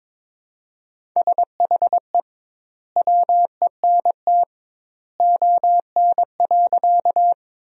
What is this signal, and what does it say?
Morse code, heard as a keyed tone, about 22 wpm, 700 Hz. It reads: SHE WENT ON.
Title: 2x Word-Spacing - Sentences from Top 200 Words - 22wpm